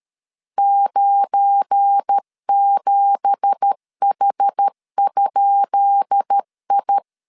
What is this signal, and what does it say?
Morse code, heard as a keyed tone, about 13 wpm, 800 Hz. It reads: 97H?I
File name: LEZIONI / 10 SIGNS / 10B INT 12wpm 2414